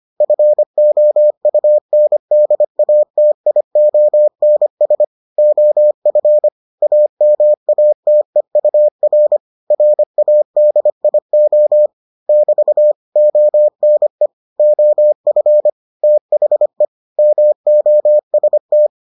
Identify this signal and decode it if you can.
Morse code, heard as a keyed tone, about 25 wpm, 600 Hz. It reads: FOUNDATIONS OF AMATEUR RADIO <BT> ONE OF THE MOST